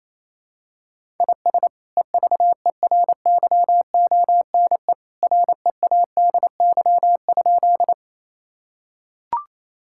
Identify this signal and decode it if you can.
Morse code, heard as a keyed tone, about 28 wpm, 700 Hz. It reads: IS EVERYONE READY?
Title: Sentences from Top 1000 Words - Review Part 5 - 28wpm